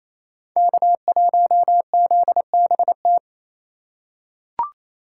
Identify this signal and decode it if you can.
Morse code, heard as a keyed tone, about 28 wpm, 690 Hz. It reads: K1ZBT
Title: USA Call Signs - 5 Characters - 4x Character Spacing - 28wpm